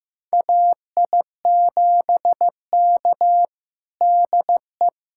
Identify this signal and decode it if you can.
Morse code, heard as a keyed tone, about 15 wpm, 700 Hz. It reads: AI7K DE
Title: USA Callsign DE Callsign - 4 Characters - 15wpm